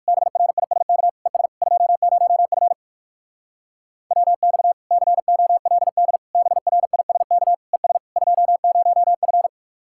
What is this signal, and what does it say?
Morse code, heard as a keyed tone, about 59 wpm, 700 Hz. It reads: BRISK ES 20F WX COLD BRISK ES 20F